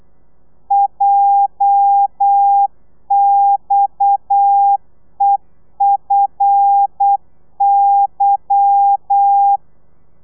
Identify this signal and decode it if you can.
Morse code, heard as a keyed tone, about 8 wpm, 800 Hz. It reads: JXEFY
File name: Lez11